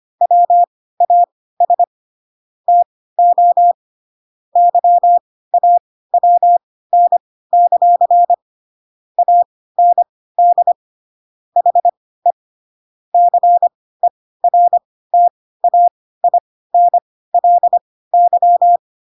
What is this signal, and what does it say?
Morse code, heard as a keyed tone, about 25 wpm, spaced as for 16 wpm, 700 Hz. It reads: WAS TO YAWN; AND HE CERTAINLY